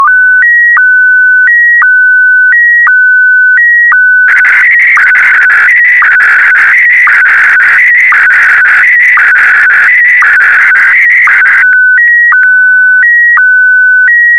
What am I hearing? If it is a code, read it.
SSTV (slow-scan television), Scottie DX.